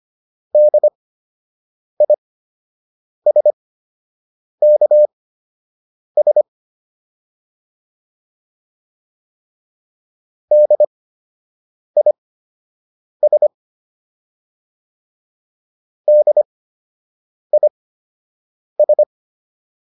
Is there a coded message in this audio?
Yes: Morse code from 0.5 to 19.0 s, heard as a keyed tone, about 25 wpm, spaced as for 7 wpm, 600 Hz. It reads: DISKS DIS DIS